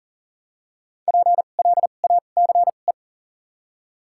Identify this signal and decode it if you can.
Morse code, heard as a keyed tone, about 40 wpm, spaced as for 29 wpm, 700 Hz. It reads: PLACE
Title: 2x Character Spacing - Top 500 Words - 40wpm